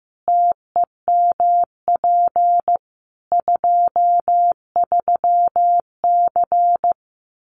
Morse code, heard as a keyed tone, about 15 wpm, 700 Hz. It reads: TEMP 23C
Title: Common QSO Phrases 15wpm